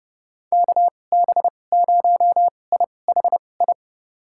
Morse code, heard as a keyed tone, about 30 wpm, spaced as for 22 wpm, 700 Hz. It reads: KB0IHI
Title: USA Call Signs - 2x Character Spacing - 30wpm